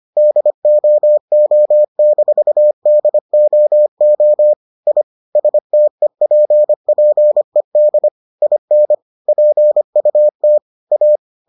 Morse code, heard as a keyed tone, about 25 wpm, 600 Hz. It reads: DOO-DOO I STEPPED IN PUT A